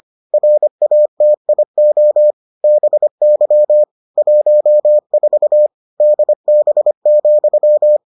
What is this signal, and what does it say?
Morse code, heard as a keyed tone, about 25 wpm, 600 Hz. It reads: RATIO BY 14 DB,